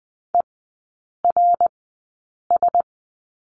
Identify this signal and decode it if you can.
Morse code, heard as a keyed tone, about 20 wpm, spaced as for 8 wpm, 700 Hz. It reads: ERS
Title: 060 Lesson 11e - Letter h - Three Characters 20wpm